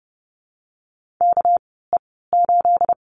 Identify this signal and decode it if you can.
Morse code, heard as a keyed tone, about 30 wpm, 700 Hz. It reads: K E 8
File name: Lesson 37g - Letter x - Call Signs 30wpm